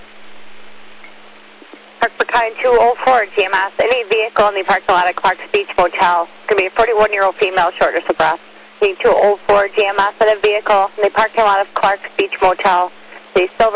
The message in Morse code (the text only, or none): none